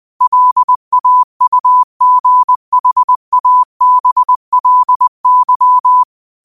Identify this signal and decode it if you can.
Morse code, heard as a keyed tone, about 20 wpm, 1000 Hz. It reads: LAUGHABLY
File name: qotdfiles / 20wpm